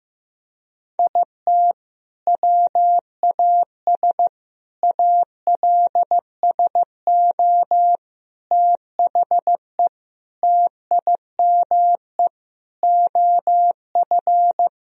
Morse code, heard as a keyed tone, about 15 wpm, 700 Hz. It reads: IT WAS ALSO THE TIME OF
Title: Sentences from Top 1000 Words - Review Part 1 - 15wpm